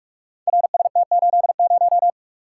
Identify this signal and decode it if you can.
Morse code, heard as a keyed tone, about 45 wpm, 700 Hz. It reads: RST80
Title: Code Groups - 5 Letter:Numbers - 45wpm